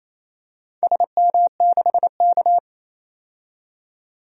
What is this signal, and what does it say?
Morse code, heard as a keyed tone, about 28 wpm, 700 Hz. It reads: SM6K
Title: DXCC-100 Call Signs - 4 Characters - 28wpm